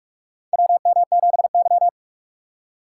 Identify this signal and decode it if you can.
Morse code, heard as a keyed tone, about 46 wpm, 700 Hz. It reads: WM7Y
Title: USA Call Signs - 4 Characters - 2x Character Spacing - 45wpm